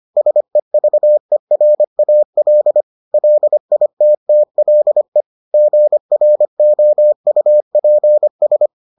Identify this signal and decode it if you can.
Morse code, heard as a keyed tone, about 25 wpm, 600 Hz. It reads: SEVERAL LITTLE GROUPS